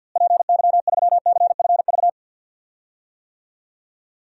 Morse code, heard as a keyed tone, about 50 wpm, 700 Hz. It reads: PY3CFV
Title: DXCC-100 Call Signs - 6 Characters - Repeated 2x - 50wpm